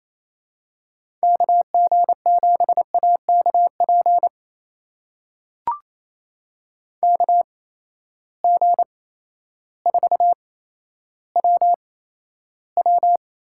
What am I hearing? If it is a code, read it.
Morse code, heard as a keyed tone, about 28 wpm, spaced as for 13 wpm, 700 Hz. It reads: KG7AKP KG4WW
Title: USA Call Signs - 6 Characters - 8x Character Spacing - 28wpm